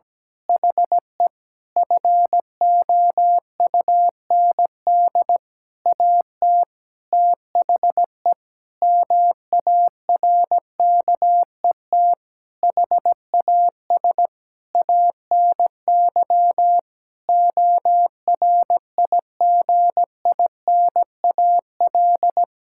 Morse code, heard as a keyed tone, about 17 wpm, 700 Hz. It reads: HE FOUND AT THE MARKET HAS ANY ORIGINAL
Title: Sentences from Top 1000 Words - Review Part 6 - 17wpm